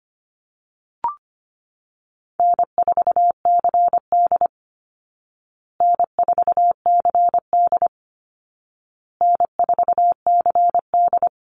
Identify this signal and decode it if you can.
Morse code, heard as a keyed tone, about 25 wpm, 700 Hz. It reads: N4CD N4CD N4CD
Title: Call Signs Repeated 2x - 25wpm